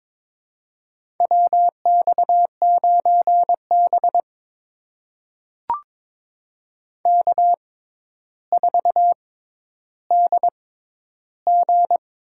Morse code, heard as a keyed tone, about 22 wpm, spaced as for 11 wpm, 700 Hz. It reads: WX9B K4DG
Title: USA Call Signs - 4 Characters - 6x Character Spacing - 22wpm